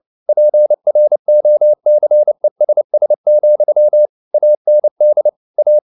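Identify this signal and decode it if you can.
Morse code, heard as a keyed tone, about 29 wpm, 600 Hz. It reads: PROCESS, AND A
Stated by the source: Morse code at 29 wpm, sending Ken Follett's novel Never in English, from CW med Gnister.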